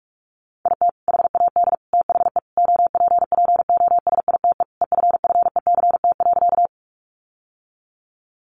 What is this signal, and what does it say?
Morse code, heard as a keyed tone, about 44 wpm, 700 Hz. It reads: IT HAD THE OPPOSITE EFFECT.
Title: Sentences from Top 1300 Words 45wpm